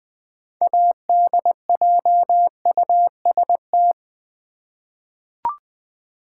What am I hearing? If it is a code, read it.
Morse code, heard as a keyed tone, about 21 wpm, 700 Hz. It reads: ADJUST
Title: 6 Letter Words - First 3 Letters Given - 20wpm